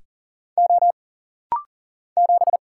Morse code, heard as a keyed tone, about 39 wpm, 700 Hz. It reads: O 7
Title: Single Letter-Number - Rapid-Fire 40wpm